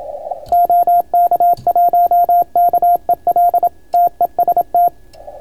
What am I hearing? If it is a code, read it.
Morse code, heard as a keyed tone, about 26 wpm, 690 Hz, with radio noise. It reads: OK1KEL TEST